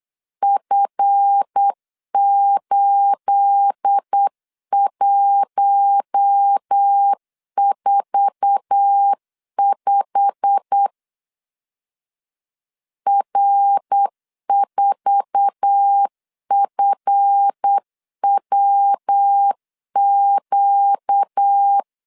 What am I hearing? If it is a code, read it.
Morse code, heard as a keyed tone, about 8 wpm, 800 Hz. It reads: F8145 R4FWQ
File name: LEZIONI / 9 28 8wpm 3512